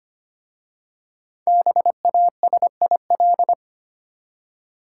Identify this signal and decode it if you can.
Morse code, heard as a keyed tone, about 25 wpm, 700 Hz. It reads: BASIL
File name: practice / 25wpm / Easy Words with Hints - 25wpm